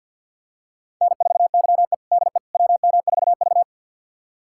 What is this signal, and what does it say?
Morse code, heard as a keyed tone, about 52 wpm, 700 Hz. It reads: N4YE DE WM4V
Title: USA Callsign DE Callsign - 4 Characters - 50wpm